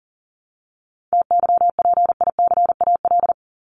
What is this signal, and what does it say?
Morse code, heard as a keyed tone, about 40 wpm, 700 Hz. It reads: TYPICAL
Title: Top 1001-1250 Words 40wpm